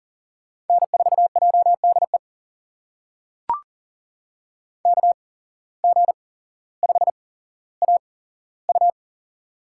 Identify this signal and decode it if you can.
Morse code, heard as a keyed tone, about 40 wpm, spaced as for 17 wpm, 700 Hz. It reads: N4JDE KG5AU
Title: USA Call Signs - 5 Characters - 8x Character Spacing - 40wpm